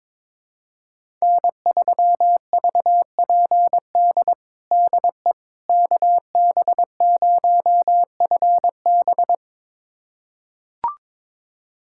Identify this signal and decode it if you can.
Morse code, heard as a keyed tone, about 22 wpm, 700 Hz. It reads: N3VPD DE KB0FB
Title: USA Callsign DE Callsign - 5 Characters - 22wpm